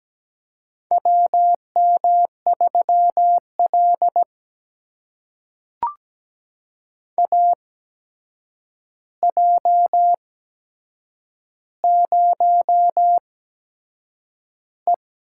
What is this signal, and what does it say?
Morse code, heard as a keyed tone, about 17 wpm, spaced as for 7 wpm, 700 Hz. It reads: WM3L AJ0E
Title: USA Call Signs - 4 Characters - 8x Character Spacing - 17wpm